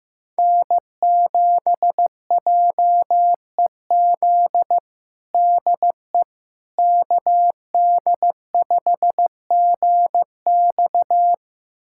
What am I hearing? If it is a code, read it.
Morse code, heard as a keyed tone, about 15 wpm, 700 Hz. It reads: N7JEZ DE KD5GX
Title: USA Callsign DE Callsign - 5 Characters - 15wpm